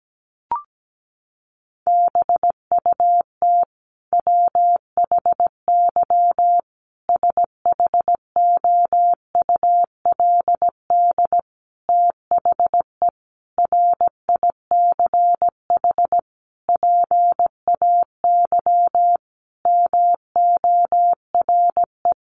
Morse code, heard as a keyed tone, about 17 wpm, 700 Hz. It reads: BUT WHY SHOULD THE RICH PAY MORE
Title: Sentences from Top 1100 Words 17wpm